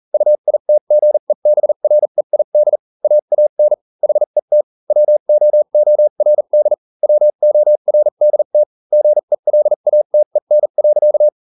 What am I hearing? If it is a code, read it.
Morse code, heard as a keyed tone, about 40 wpm, spaced as for 35 wpm, 600 Hz. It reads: UITGEBREID AAN HET WOORD WORDT GELATEN.